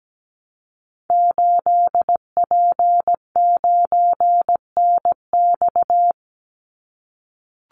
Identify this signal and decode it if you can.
Morse code, heard as a keyed tone, about 17 wpm, 700 Hz. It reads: 8P9NX
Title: Call Signs - 17wpm